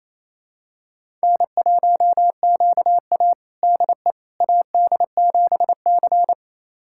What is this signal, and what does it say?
Morse code, heard as a keyed tone, about 28 wpm, 700 Hz. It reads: N1QA DE AD7C
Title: USA Callsign DE Callsign - 4 Characters - 28wpm